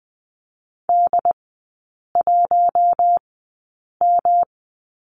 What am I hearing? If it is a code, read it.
Morse code, heard as a keyed tone, about 20 wpm, spaced as for 8 wpm, 700 Hz. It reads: D1M
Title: 121 Lesson 20e - Number 3 - Three Characters 20wpm